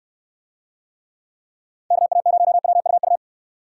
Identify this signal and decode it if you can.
Morse code, heard as a keyed tone, about 69 wpm, 700 Hz. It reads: KA0WUU